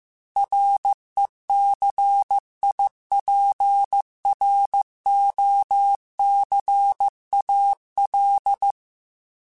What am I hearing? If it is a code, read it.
Morse code, heard as a keyed tone, about 15 wpm, 800 Hz. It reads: RECIPROCAL